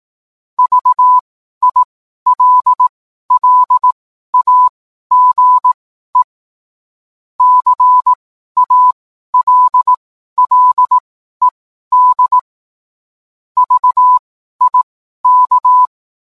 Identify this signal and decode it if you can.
Morse code, heard as a keyed tone, about 18 wpm, spaced as for 12 wpm, 1000 Hz. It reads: VILLAGE CALLED VIK
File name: cwblog / cw mp3 directory / mem morse 12